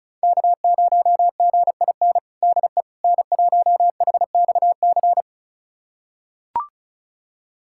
Morse code, heard as a keyed tone, about 36 wpm, 700 Hz. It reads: K0GIN DE N1HXC
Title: USA Callsign DE Callsign - 5 Characters - 35wpm